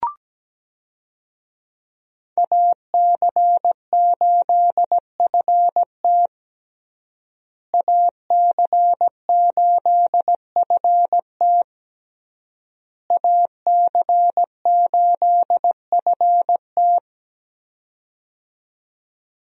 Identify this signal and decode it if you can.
Morse code, heard as a keyed tone, about 17 wpm, 700 Hz. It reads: AC8FT AC8FT AC8FT
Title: USA Call Signs - 5 Characters - Repeated 3x - 17wpm